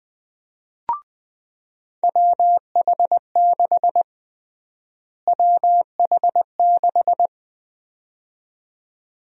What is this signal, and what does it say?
Morse code, heard as a keyed tone, about 20 wpm, 700 Hz. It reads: WH6 WH6
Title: USA Call Sign Prefixes - Repeated 2x - 20wpm